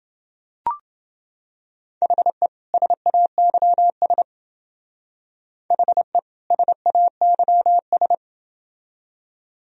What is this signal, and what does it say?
Morse code, heard as a keyed tone, about 28 wpm, 700 Hz. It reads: HE SAYS HE SAYS